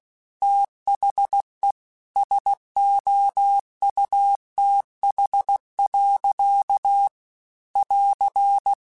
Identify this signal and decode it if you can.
Morse code, heard as a keyed tone, about 16 wpm, 800 Hz. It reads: THE SOUTH. <AR>
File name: Weather Sets / Weather Sets-mp3s / Weather 4 10 @15